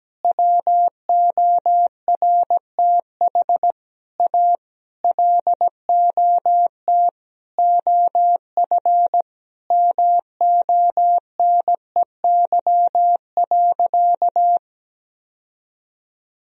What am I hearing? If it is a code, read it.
Morse code, heard as a keyed tone, about 17 wpm, 700 Hz. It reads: WORTH A LOT OF MONEY.
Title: Sentences from Top 1000 Words - Review Part 4 - 17wpm